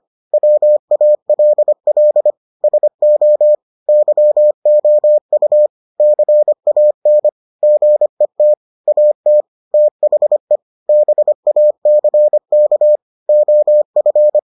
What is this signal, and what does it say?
Morse code, heard as a keyed tone, about 25 wpm, 600 Hz. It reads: WALL SO YOU CAN GET AT THE BACK OF